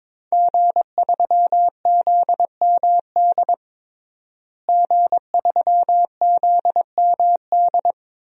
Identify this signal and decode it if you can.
Morse code, heard as a keyed tone, about 22 wpm, 700 Hz. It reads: G3ZMD G3ZMD
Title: DXCC-100 Call Signs - 5 Characters - Repeated 2x - 22wpm